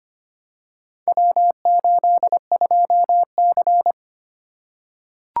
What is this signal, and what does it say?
Morse code, heard as a keyed tone, about 25 wpm, 700 Hz. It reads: W82C
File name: practice / 25wpm / Four Letter-Number Combinations - 25wpm